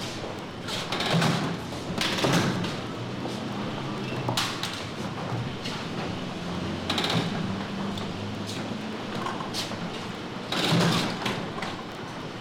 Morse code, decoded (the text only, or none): none